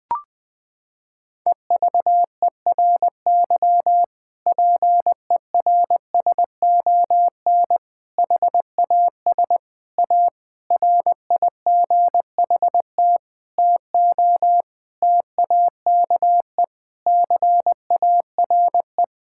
Morse code, heard as a keyed tone, about 20 wpm, 700 Hz. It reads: EVERY PERSON HAS A RIGHT TO TAKE CARE
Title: Sentences from Top 1000 Words - Review Part 5 - 20wpm